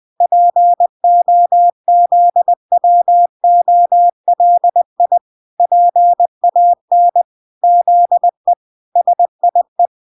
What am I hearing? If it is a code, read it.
Morse code, heard as a keyed tone, about 20 wpm, 700 Hz. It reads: POZWOLI PAN ZE SIE
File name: 20wpm